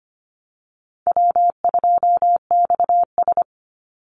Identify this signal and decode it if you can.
Morse code, heard as a keyed tone, about 25 wpm, 700 Hz. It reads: W2XS
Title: Call Signs Repeated 3x - 25wpm